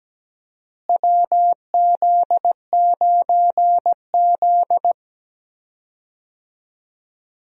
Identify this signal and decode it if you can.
Morse code, heard as a keyed tone, about 17 wpm, 700 Hz. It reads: WZ9Z
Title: USA Call Signs - 4 Characters - Repeated 3x - 17wpm